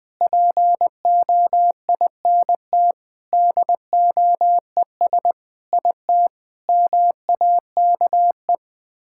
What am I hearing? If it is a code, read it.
Morse code, heard as a keyed tone, about 20 wpm, 700 Hz. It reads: POINT DOES IT MAKE
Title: Sets of 8 Words - 20wpm